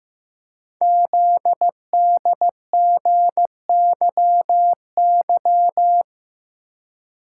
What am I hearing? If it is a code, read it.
Morse code, heard as a keyed tone, about 15 wpm, 700 Hz. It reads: ZDGYY
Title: Code Groups - 5 Letters - 15wpm